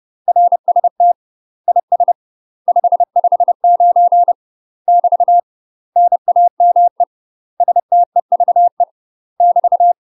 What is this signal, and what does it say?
Morse code, heard as a keyed tone, about 30 wpm, 700 Hz. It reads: RST IS 559 <BT> NAME STEVE <BT>
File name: UK MASTER QSOs TRAINING / 30WPM / UK MASTER QSOs 0029